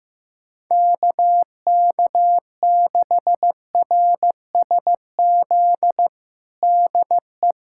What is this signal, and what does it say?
Morse code, heard as a keyed tone, about 16 wpm, 700 Hz. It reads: KK6RSZ DE
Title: USA Callsign DE Callsign 15wpm